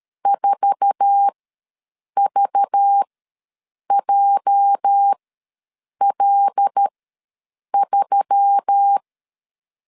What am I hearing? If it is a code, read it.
Morse code, heard as a keyed tone, about 13 wpm, spaced as for 7 wpm, 800 Hz. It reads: 4VJL3
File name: LEZIONI / 8 YJ45 / 8 YJ4 ws 12wpm 3249